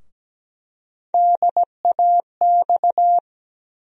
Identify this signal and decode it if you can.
Morse code, heard as a keyed tone, about 17 wpm, 700 Hz. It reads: DAX
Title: USA Call Sign Postfixes - 3 Letters 17wpm